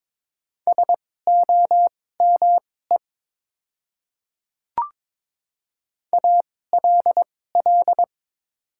Morse code, heard as a keyed tone, about 22 wpm, spaced as for 16 wpm, 700 Hz. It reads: SOME ALL
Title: 2x Character Spacing - Top 100 Words - 22wpm